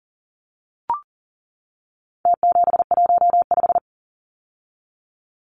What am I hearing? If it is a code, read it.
Morse code, heard as a keyed tone, about 40 wpm, 700 Hz. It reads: T715
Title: Numbers - Contest Style - 4-Digit 40wpm